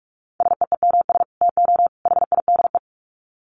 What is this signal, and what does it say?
Morse code, heard as a keyed tone, about 44 wpm, 700 Hz. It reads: SEEMS TO HIDE
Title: Dit Challenge 45wpm